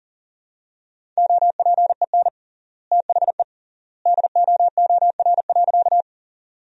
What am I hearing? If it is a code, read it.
Morse code, heard as a keyed tone, about 40 wpm, 700 Hz. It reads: OPEN THE DOOR.
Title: 3x Word-Spacing - Sentences from Top 300 Words - 40wpm